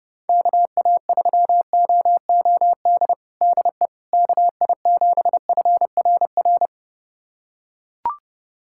Morse code, heard as a keyed tone, about 30 wpm, 690 Hz. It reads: KA3OOD DE KI7FRR
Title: USA Callsign DE Callsign 30wpm